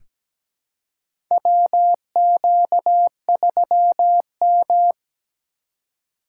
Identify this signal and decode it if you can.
Morse code, heard as a keyed tone, about 17 wpm, 700 Hz. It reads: WQ3M
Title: USA Call Signs - 4 Characters - 2x Character Spacing - 17wpm